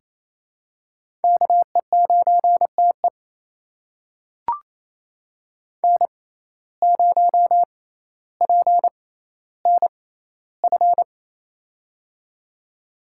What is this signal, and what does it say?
Morse code, heard as a keyed tone, about 28 wpm, spaced as for 14 wpm, 700 Hz. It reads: KE9TE N0PNF
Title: USA Call Signs - 5 Characters - 6x Character Spacing - 28wpm